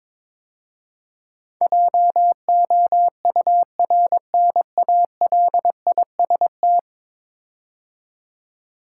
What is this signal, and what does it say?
Morse code, heard as a keyed tone, about 21 wpm, 700 Hz. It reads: JOURNALIST
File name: practice / 22wpm / 10 Letter Words - First 3 Letters Given - 22wpm